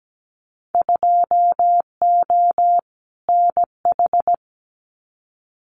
Morse code, heard as a keyed tone, about 17 wpm, 700 Hz. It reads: 2O NH
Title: Winter Field Day Exchanges 17wpm